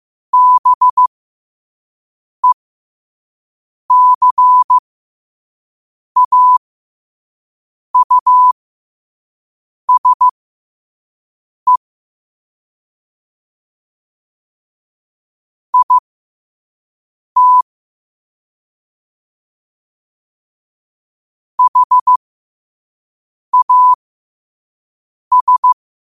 Morse code, heard as a keyed tone, about 15 wpm, spaced as for 5 wpm, 1000 Hz. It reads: BECAUSE IT HAS
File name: qotdfiles / 5wpm